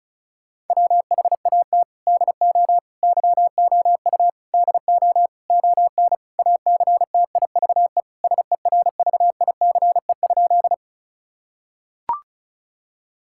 Morse code, heard as a keyed tone, about 34 wpm, 700 Hz. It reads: WHAT DO YOU DO ON ACTIVE SERVICE?
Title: Sentences from Top 1000 Words - Review Part 6 - 35wpm